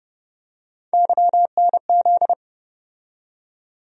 Morse code, heard as a keyed tone, about 30 wpm, 700 Hz. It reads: YNZ
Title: USA Call Sign Postfixes - 3 Letters 30wpm